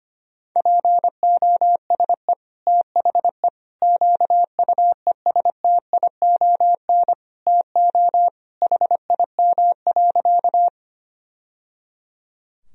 Morse code, heard as a keyed tone, about 25 wpm, 700 Hz. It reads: POSE THE QUESTION TO HIM.
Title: Sentences from Top 1000 Words - Review Part 2 - 25wpm